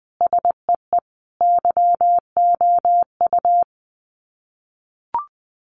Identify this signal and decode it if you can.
Morse code, heard as a keyed tone, about 20 wpm, 700 Hz. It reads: SEE YOU
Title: Sets of 4 Words 20wpm